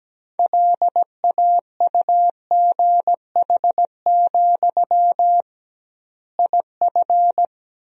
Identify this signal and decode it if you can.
Morse code, heard as a keyed tone, about 17 wpm, 700 Hz. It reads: LAUGH, IF
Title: 2x Word-Spacing - Sentences from Top 500 Words - 17wpm